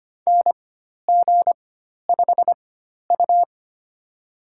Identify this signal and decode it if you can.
Morse code, heard as a keyed tone, about 25 wpm, spaced as for 12 wpm, 700 Hz. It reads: NG5U
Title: USA Call Signs - 4x Character Spacing - 25wpm